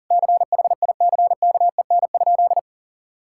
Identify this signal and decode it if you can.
Morse code, heard as a keyed tone, about 40 wpm, 700 Hz. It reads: CHICKEN?
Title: Sentences from Top 1000 Words - Review Part 5 - 40wpm